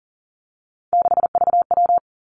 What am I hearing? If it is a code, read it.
Morse code, heard as a keyed tone, about 41 wpm, 700 Hz. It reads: 6VW
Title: Three Challenging Letter-Number Combinations 40wpm